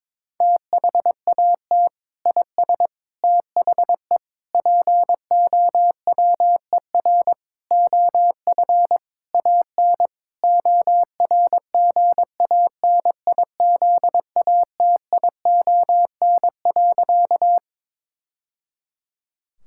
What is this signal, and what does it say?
Morse code, heard as a keyed tone, about 22 wpm, 700 Hz. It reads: THAT IS THE POWER OF AN ORGANIZATION.
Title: Sentences from Top 1000 Words - Review Part 5 - 22wpm